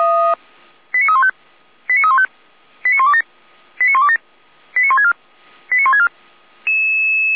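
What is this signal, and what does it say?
Five-tone selective call (ZVEI), heard as quick runs of five tones, tones of about 70 ms. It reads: 89315 89315 89127 89127 89164 89164
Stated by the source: ZVEI five-tone selective call received by SDR, from the Feuerwehr Orlamünde audio server.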